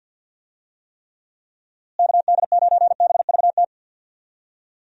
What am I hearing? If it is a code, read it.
Morse code, heard as a keyed tone, about 52 wpm, 700 Hz. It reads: KD9BVT
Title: USA Call Signs - 6 Characters - Repeated 2x - 50wpm